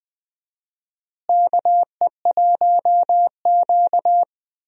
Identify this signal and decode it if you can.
Morse code, heard as a keyed tone, about 21 wpm, 700 Hz. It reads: KE1Q
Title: USA Call Signs - 4 Characters - Repeated 2x - 20wpm